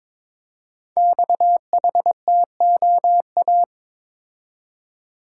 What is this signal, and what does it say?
Morse code, heard as a keyed tone, about 22 wpm, 690 Hz. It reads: XHTOA
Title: Code Groups - 5 Letters - 22wpm